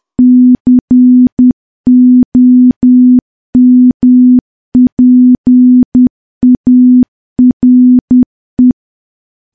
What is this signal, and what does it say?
Morse code, heard as a keyed tone, about 10 wpm, 250 Hz. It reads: COMPARE
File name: morse / morse-training-10WPM